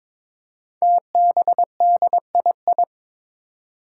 Morse code, heard as a keyed tone, about 22 wpm, 700 Hz. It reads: TBDII
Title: Five Challenging Letter-Number Combinations 22wpm